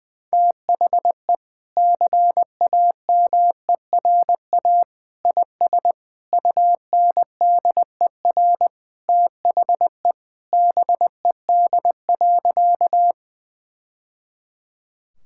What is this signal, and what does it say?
Morse code, heard as a keyed tone, about 20 wpm, 700 Hz. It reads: THE CAMERA IS UNDER THE BED.
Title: Sentences from Top 1000 Words - Review Part 5 - 20wpm